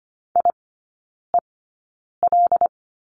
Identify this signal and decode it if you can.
Morse code, heard as a keyed tone, about 25 wpm, spaced as for 9 wpm, 700 Hz. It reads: IEL
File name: course / 25:10wpm / 093 Lesson 16e - Letter c - Three Characters 25wpm